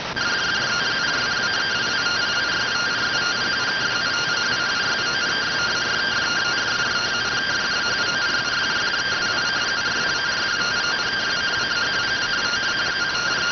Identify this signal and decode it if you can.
Two-tone teleprinter signal, about 170 Hz shift.